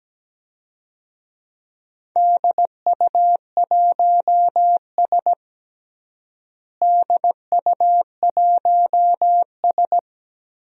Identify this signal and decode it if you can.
Morse code, heard as a keyed tone, about 17 wpm, 700 Hz. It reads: DU1S DU1S
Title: DXCC-100 Call Signs - 4 Characters - Repeated 2x - 17wpm